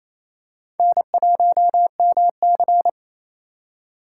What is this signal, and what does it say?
Morse code, heard as a keyed tone, about 28 wpm, 700 Hz. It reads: N1MC
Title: USA Call Signs - 4 Characters - 4x Character Spacing - 28wpm